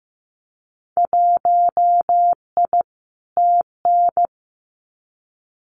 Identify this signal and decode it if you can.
Morse code, heard as a keyed tone, about 15 wpm, 700 Hz. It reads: 1I TN
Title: Winter Field Day Exchanges 15wpm